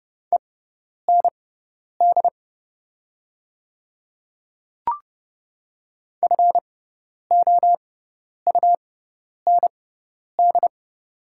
Morse code, heard as a keyed tone, about 30 wpm, spaced as for 10 wpm, 700 Hz. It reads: END FOUND
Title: 6x Character Spacing - Top 300 Words - 30wpm